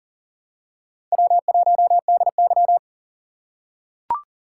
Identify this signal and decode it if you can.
Morse code, heard as a keyed tone, about 38 wpm, 700 Hz. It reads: W1DY
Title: USA Call Signs - 2x Character Spacing - 40wpm